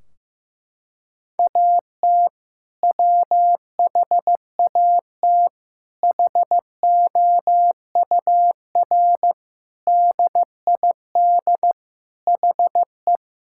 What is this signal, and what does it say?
Morse code, heard as a keyed tone, about 16 wpm, 700 Hz. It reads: AT WHAT HOUR DID HE
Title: Sentences from Top 1600 Words 15wpm